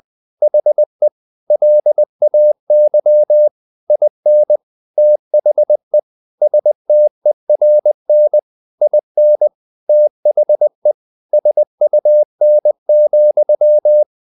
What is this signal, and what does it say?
Morse code, heard as a keyed tone, about 20 wpm, 600 Hz. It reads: HE LAY IN THE STERN IN THE SUN,
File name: the old man and the sea / the old man and the sea part 1 20wpm